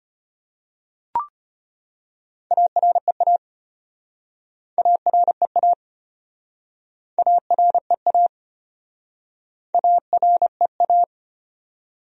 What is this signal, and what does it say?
Morse code, heard as a keyed tone, about 33 wpm, 700 Hz. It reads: AREA AREA AREA AREA